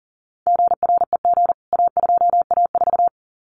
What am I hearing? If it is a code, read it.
Morse code, heard as a keyed tone, about 40 wpm, 700 Hz. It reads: GREG A2A4
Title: CWT Exchange 40wpm